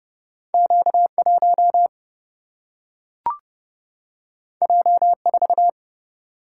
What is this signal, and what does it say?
Morse code, heard as a keyed tone, about 30 wpm, 700 Hz. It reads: Q 1 J 4
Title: Single-Letter Single-Number 30wpm